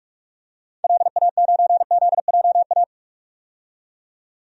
Morse code, heard as a keyed tone, about 45 wpm, 700 Hz. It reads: LA9ZJA
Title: DXCC-100 Call Signs - 6 Characters - 45wpm